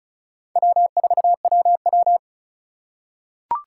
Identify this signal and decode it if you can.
Morse code, heard as a keyed tone, about 36 wpm, 700 Hz. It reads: W4WW
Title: USA Call Signs - 4 Characters - 4x Character Spacing - 35wpm